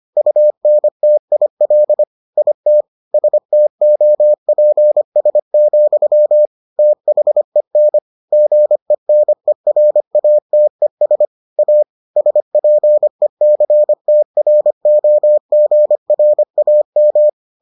Morse code, heard as a keyed tone, about 25 wpm, 600 Hz. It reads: UNTIL IT STOPS, THEN GENERATES A SPECTROGRAM